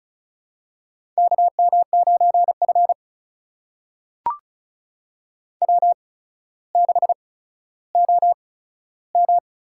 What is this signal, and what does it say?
Morse code, heard as a keyed tone, about 34 wpm, spaced as for 15 wpm, 700 Hz. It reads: KM9F W6OM
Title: USA Call Signs - 4 Characters - 8x Character Spacing - 35wpm